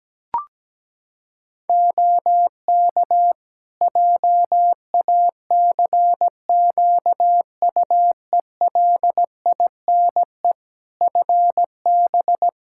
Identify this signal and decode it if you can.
Morse code, heard as a keyed tone, about 17 wpm, 700 Hz. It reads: OK JACQUELINE FB
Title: K1USN SST QSO Practice 17wpm